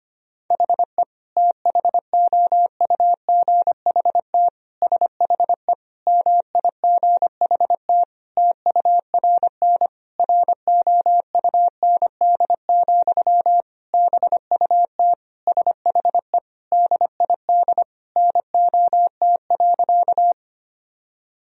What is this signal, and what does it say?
Morse code, heard as a keyed tone, about 25 wpm, 700 Hz. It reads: HE THOUGHT SHE MIGHT TURN ROUND, BUT SHE DID NOT.
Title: Sentences from Top 1000 Words - Review Part 1 - 25wpm